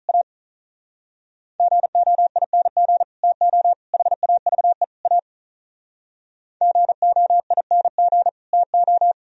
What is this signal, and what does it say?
Morse code, heard as a keyed tone, about 45 wpm, 700 Hz. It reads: A GOING TO HAVE A GOING TO